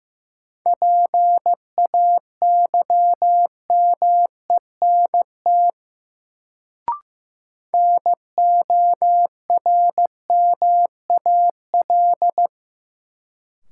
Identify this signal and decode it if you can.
Morse code, heard as a keyed tone, about 15 wpm, 700 Hz. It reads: PAYMENT NORMAL
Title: Sentences from Top 1200 Words - New Word Drill - 15wpm